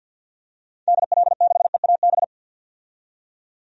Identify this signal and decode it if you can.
Morse code, heard as a keyed tone, about 52 wpm, 700 Hz. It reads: DL6EAB
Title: DXCC-100 Call Signs - 6 Characters - 50wpm